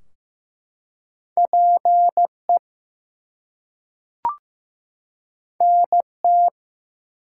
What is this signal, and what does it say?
Morse code, heard as a keyed tone, about 15 wpm, 700 Hz. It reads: PE NT
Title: US States and Canadian Provinces 15wpm